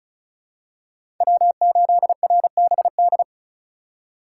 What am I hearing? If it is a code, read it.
Morse code, heard as a keyed tone, about 36 wpm, 700 Hz. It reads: W8RBD
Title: USA Call Signs - 5 Characters - 3x Character Spacing - 35wpm